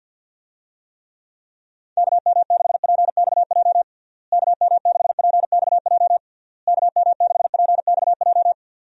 Morse code, heard as a keyed tone, about 52 wpm, 700 Hz. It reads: KM6PXJ KM6PXJ KM6PXJ
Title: USA Call Signs - 6 Characters - Repeated 3x - 50wpm